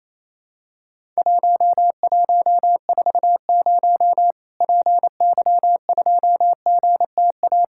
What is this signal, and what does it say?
Morse code, heard as a keyed tone, about 28 wpm, 700 Hz. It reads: 1140 PY2GTA
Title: Worked All Europe DX Contest - QTC Exchange - 28wpm